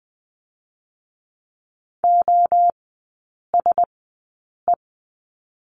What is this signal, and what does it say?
Morse code, heard as a keyed tone, about 20 wpm, 700 Hz. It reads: O S E
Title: 047 Lesson 9e - Number 4 - Three Characters 20wpm